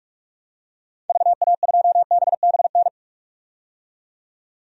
Morse code, heard as a keyed tone, about 46 wpm, 700 Hz. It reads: VA2BBN
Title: DXCC-100 Call Signs - 6 Characters - 45wpm